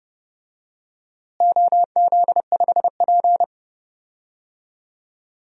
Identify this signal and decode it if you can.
Morse code, heard as a keyed tone, about 30 wpm, 700 Hz. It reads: OZ5P
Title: DXCC-100 Call Signs - 4 Characters - Repeated 2x - 30wpm